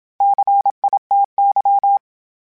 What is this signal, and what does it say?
Morse code, heard as a keyed tone, about 27 wpm, 800 Hz. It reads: CITY
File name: fromSMs / FromSMs mp3s / Cities@25